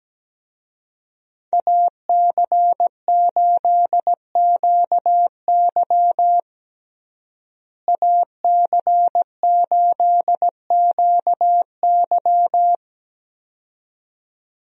Morse code, heard as a keyed tone, about 17 wpm, 700 Hz. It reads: AC8QY AC8QY
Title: USA Call Signs - 5 Characters - Repeated 2x - 17wpm